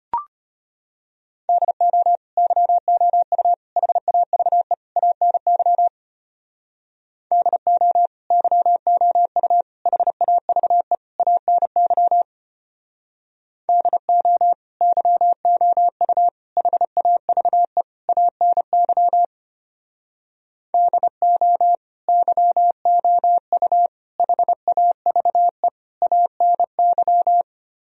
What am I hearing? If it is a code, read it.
Morse code, heard as a keyed tone, about 33 wpm, 700 Hz. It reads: DO YOU HAVE ANY DO YOU HAVE ANY DO YOU HAVE ANY DO YOU HAVE ANY